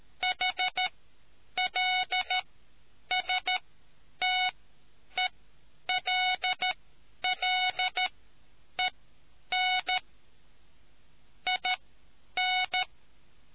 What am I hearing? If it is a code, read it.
Morse code, heard as a keyed tone, about 13 wpm, spaced as for 9 wpm, 2200 Hz. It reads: HLSTELLEN IN